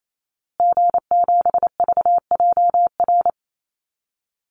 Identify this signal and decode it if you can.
Morse code, heard as a keyed tone, about 28 wpm, 700 Hz. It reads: G7VJR
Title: Call Signs Repeated 2x - 28wpm